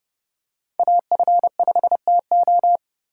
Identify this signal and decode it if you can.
Morse code, heard as a keyed tone, about 32 wpm, 700 Hz. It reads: AF5TO